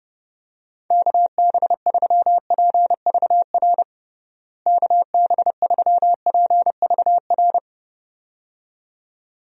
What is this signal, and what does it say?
Morse code, heard as a keyed tone, about 30 wpm, 700 Hz. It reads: KB3PVR KB3PVR
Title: USA Call Signs - 6 Characters - Repeated 2x - 30wpm